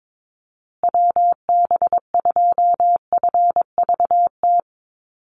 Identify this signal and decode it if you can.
Morse code, heard as a keyed tone, about 22 wpm, 690 Hz. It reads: WB2FVT
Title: USA Call Signs Repeated 3x 22wpm